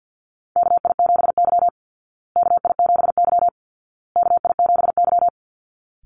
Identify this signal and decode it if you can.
Morse code, heard as a keyed tone, about 50 wpm, 700 Hz. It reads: KI7Y KI7Y KI7Y
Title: Call Signs Repeated 2x - 50wpm